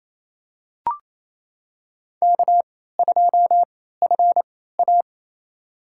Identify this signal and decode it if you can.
Morse code, heard as a keyed tone, about 28 wpm, spaced as for 16 wpm, 700 Hz. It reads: K2FA
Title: USA Call Signs - 4 Characters - 3x Character Spacing - 28wpm